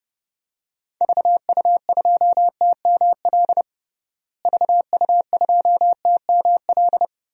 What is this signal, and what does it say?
Morse code, heard as a keyed tone, about 32 wpm, 700 Hz. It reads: VU2TML VU2TML
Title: DXCC-100 Call Signs - 6 Characters - Repeated 2x - 30wpm